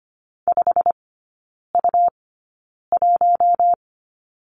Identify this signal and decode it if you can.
Morse code, heard as a keyed tone, about 25 wpm, spaced as for 9 wpm, 700 Hz. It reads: 5U1
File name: Lesson 17e - Letter u - Three Characters 25wpm